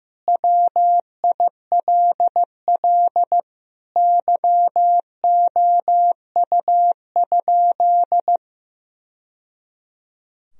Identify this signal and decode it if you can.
Morse code, heard as a keyed tone, about 15 wpm, 700 Hz. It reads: WILL YOU?
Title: Sentences from Top 1000 Words - Review Part 1 - 15wpm